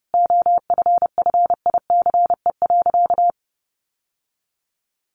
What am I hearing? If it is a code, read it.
Morse code, heard as a keyed tone, about 30 wpm, 700 Hz. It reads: OFFICE.
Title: Sentences from Top 1000 Words 30wpm